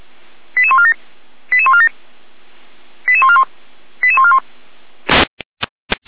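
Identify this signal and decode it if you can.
Five-tone selective call (ZVEI), heard as quick runs of five tones, tones of about 70 ms. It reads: 88147 88147 88141 88141
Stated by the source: ZVEI five-tone call received by SDR, Feuerwehr Orlamünde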